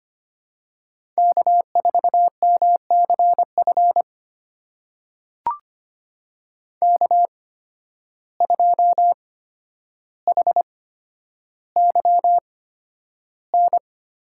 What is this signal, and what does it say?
Morse code, heard as a keyed tone, about 25 wpm, spaced as for 11 wpm, 700 Hz. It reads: K4MCF K2HYN
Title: USA Call Signs - 5 Characters - 8x Character Spacing - 25wpm